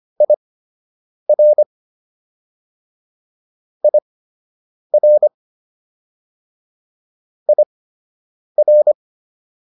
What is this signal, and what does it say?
Morse code, heard as a keyed tone, about 25 wpm, spaced as for 8 wpm, 600 Hz. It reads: IR IR IR